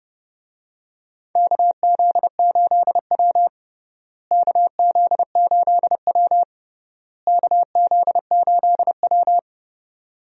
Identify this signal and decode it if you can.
Morse code, heard as a keyed tone, about 29 wpm, 700 Hz. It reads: KZ8W KZ8W KZ8W
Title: USA Call Signs - 4 Characters - Repeated 3x - 30wpm